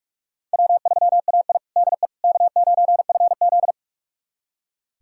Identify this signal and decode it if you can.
Morse code, heard as a keyed tone, about 44 wpm, 700 Hz. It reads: W3AI DE K9FZ
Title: USA Callsign DE Callsign - 4 Characters - 45wpm